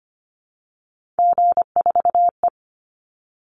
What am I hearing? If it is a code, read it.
Morse code, heard as a keyed tone, about 25 wpm, 700 Hz. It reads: G4E
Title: Call Signs Repeated 3x - 25wpm